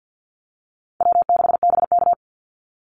Morse code, heard as a keyed tone, about 48 wpm, 700 Hz. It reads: W6BK
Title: Call Signs Repeated 3x - 50wpm